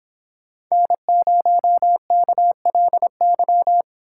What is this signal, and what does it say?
Morse code, heard as a keyed tone, about 26 wpm, 700 Hz. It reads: N0KLY